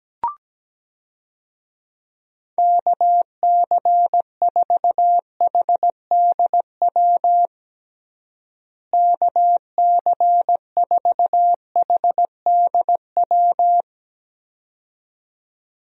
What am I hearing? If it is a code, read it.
Morse code, heard as a keyed tone, about 17 wpm, 700 Hz. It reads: KC4HDW KC4HDW
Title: USA Call Signs - 6 Characters - Repeated 2x - 17wpm